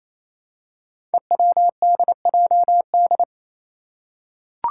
Morse code, heard as a keyed tone, about 27 wpm, 700 Hz. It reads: EWDJD
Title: Five Challenging Letter-Number Combinations 28wpm